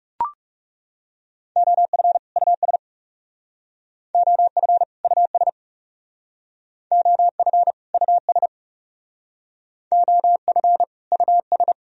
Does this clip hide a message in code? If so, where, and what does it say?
Yes: Morse code from 1.6 to 11.7 s, heard as a keyed tone, about 38 wpm, 700 Hz. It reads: OF US OF US OF US OF US